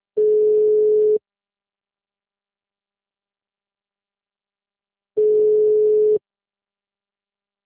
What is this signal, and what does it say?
Ringback tone (Europe, 425 Hz).